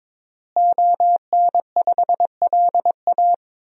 Morse code, heard as a keyed tone, about 22 wpm, 700 Hz. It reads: ON5LA
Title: DXCC-100 Call Signs - 5 Characters - Repeated 3x - 22wpm